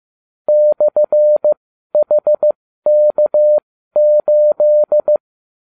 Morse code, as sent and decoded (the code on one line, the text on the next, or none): -..-. .... -.- ---..
/HK8